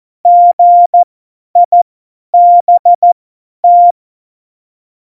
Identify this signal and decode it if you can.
Morse code, heard as a keyed tone, about 14 wpm, spaced as for 10 wpm, 700 Hz. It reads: GIBT
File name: Morsekurs / Witze / Witz 04 Juli 17, Tempo50 F70